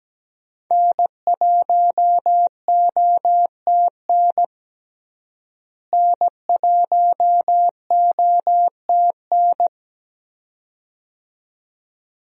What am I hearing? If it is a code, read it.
Morse code, heard as a keyed tone, about 17 wpm, 700 Hz. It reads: N1OTN N1OTN
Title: USA Call Signs - 5 Characters - Repeated 2x - 17wpm